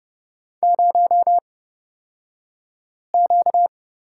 Morse code, heard as a keyed tone, about 31 wpm, spaced as for 5 wpm, 700 Hz. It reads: 0Q
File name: Lesson 38d - Letter q - Two Characters 30wpm